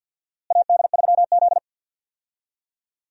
Morse code, heard as a keyed tone, about 52 wpm, 690 Hz. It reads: AD3Z
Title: USA Call Signs - 4 Characters - Repeated 2x - 50wpm